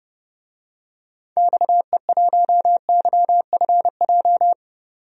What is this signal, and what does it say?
Morse code, heard as a keyed tone, about 30 wpm, 700 Hz. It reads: XE1YFJ
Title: DXCC-100 Call Signs - 6 Characters - 30wpm